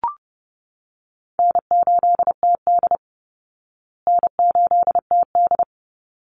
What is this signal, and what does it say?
Morse code, heard as a keyed tone, about 30 wpm, 700 Hz. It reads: N8TD N8TD
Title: Call Signs Repeated 3x - 30wpm